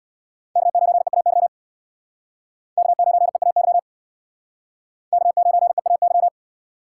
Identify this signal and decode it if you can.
Morse code, heard as a keyed tone, about 63 wpm, 700 Hz. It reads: K9EAY K9EAY K9EAY